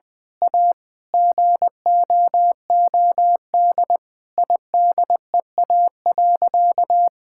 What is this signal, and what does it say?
Morse code, heard as a keyed tone, about 19 wpm, 700 Hz. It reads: A GOOD IDEA.
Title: Sentences with 4-Word Phrases - Spoken Beginning - 20wpm